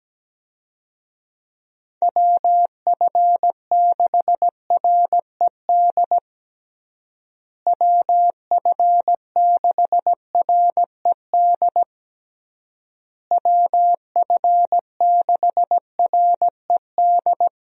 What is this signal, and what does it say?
Morse code, heard as a keyed tone, about 17 wpm, 700 Hz. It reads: WF6RED WF6RED WF6RED
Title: USA Call Signs - 6 Characters - Repeated 3x - 17wpm